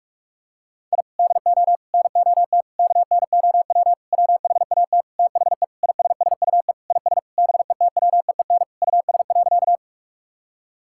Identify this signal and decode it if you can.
Morse code, heard as a keyed tone, about 46 wpm, 690 Hz. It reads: I DO NOT KNOW WHAT THE ISSUE IS BETWEEN US.
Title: Sentences from Top 1000 Words - Review Part 6 - 45wpm